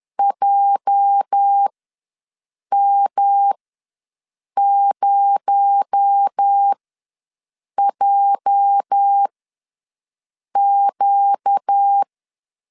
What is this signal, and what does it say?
Morse code, heard as a keyed tone, about 10 wpm, spaced as for 6 wpm, 800 Hz. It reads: JM0JQ
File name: LEZIONI / 8 YJ45 / 8 YJ ws 10wpm 4043